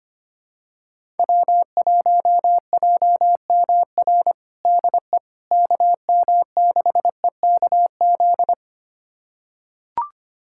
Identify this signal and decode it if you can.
Morse code, heard as a keyed tone, about 25 wpm, 700 Hz. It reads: W1JMR DE KM6EKZ
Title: USA Callsign DE Callsign 25wpm